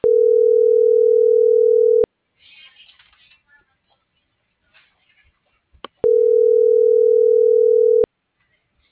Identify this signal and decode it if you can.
Ringback tone (North America), 440 + 480 Hz.